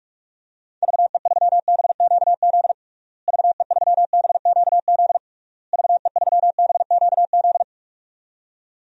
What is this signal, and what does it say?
Morse code, heard as a keyed tone, about 46 wpm, 700 Hz. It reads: VE3BQZ VE3BQZ VE3BQZ
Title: DXCC-100 Call Signs - 6 Characters - Repeated 3x - 45wpm